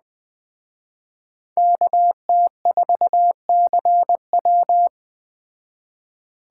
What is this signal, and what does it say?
Morse code, heard as a keyed tone, about 21 wpm, 700 Hz. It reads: KT4CW
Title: Speed Racing - USA 2x2 Call Signs 17wpm